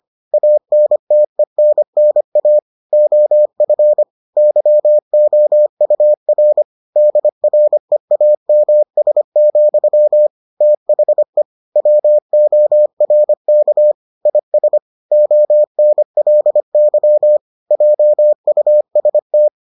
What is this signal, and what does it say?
Morse code, heard as a keyed tone, about 25 wpm, 600 Hz. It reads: ANTENNA OF YOUR DREAMS, THE WORK IS ONLY JUST